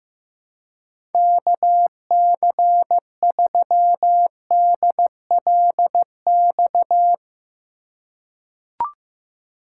Morse code, heard as a keyed tone, about 15 wpm, 700 Hz. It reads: KC3DLX
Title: USA Call Signs - 6 Characters - 2x Character Spacing - 15wpm